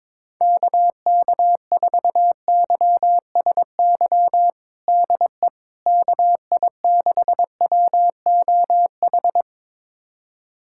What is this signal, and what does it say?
Morse code, heard as a keyed tone, about 22 wpm, 700 Hz. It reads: KK4YSY DE KI6WOH
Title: USA Callsign DE Callsign 22wpm